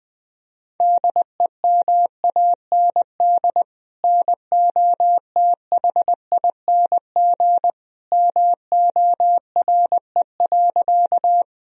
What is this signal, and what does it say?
Morse code, heard as a keyed tone, about 20 wpm, 700 Hz. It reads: DEMAND NOTHING MORE.
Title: Sentences from Top 1000 Words - Review Part 4 - 20wpm